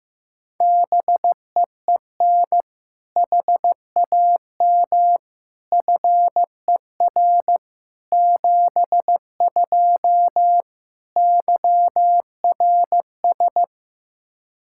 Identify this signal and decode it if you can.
Morse code, heard as a keyed tone, about 15 wpm, 700 Hz. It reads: BEEN HAM FER 72 YRS
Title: Practice QSO - Exchange 3 - More Info - 15wpm